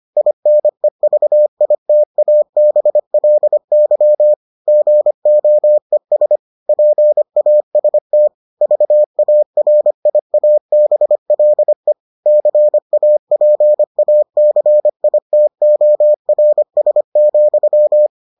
Morse code, heard as a keyed tone, about 25 wpm, 600 Hz. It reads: INEVITABLY GOES PAST VARIABLE CAPACITORS,